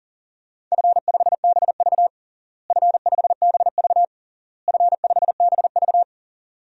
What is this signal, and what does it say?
Morse code, heard as a keyed tone, about 40 wpm, 700 Hz. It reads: F5BV F5BV F5BV
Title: DXCC-100 Call Signs - 4 Characters - Repeated 3x - 40wpm